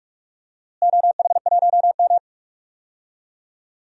Morse code, heard as a keyed tone, about 46 wpm, 700 Hz. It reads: OH1M
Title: DXCC-100 Call Signs - 4 Characters - Repeated 3x - 45wpm